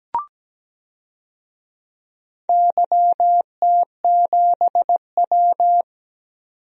Morse code, heard as a keyed tone, about 17 wpm, 700 Hz. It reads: YT7W
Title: DXCC-100 Call Signs - 4 Characters - 17wpm